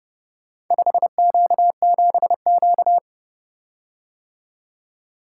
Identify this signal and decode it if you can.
Morse code, heard as a keyed tone, about 31 wpm, 700 Hz. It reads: 5Q7Q